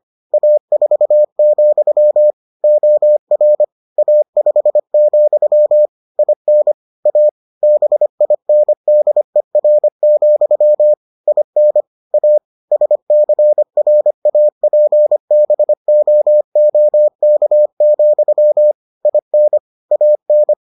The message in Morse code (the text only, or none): A4, OR A5, IN A BINDER, IN A SCRAPBOOK, IN AN